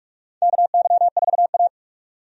Morse code, heard as a keyed tone, about 46 wpm, 700 Hz. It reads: KY4A
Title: USA Call Signs - 4 Characters - 45wpm